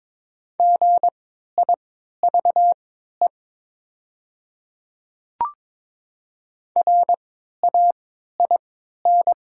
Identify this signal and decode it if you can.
Morse code, heard as a keyed tone, about 22 wpm, spaced as for 13 wpm, 700 Hz. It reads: GIVE RAIN
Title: 3x Character Spacing - Top 300 Words - 22wpm